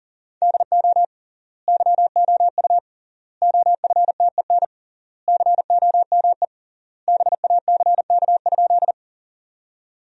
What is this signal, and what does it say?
Morse code, heard as a keyed tone, about 40 wpm, 700 Hz. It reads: DO YOU OFTEN COME BACK?
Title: 3x Word-Spacing - Sentences from Top 300 Words - 40wpm